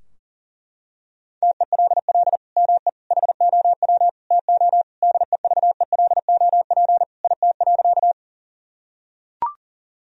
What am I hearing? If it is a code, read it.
Morse code, heard as a keyed tone, about 40 wpm, 700 Hz. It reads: TELL ME HOW TO DEVELOP IT.
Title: Sentences from Top 1000 Words - Review Part 2 - 40wpm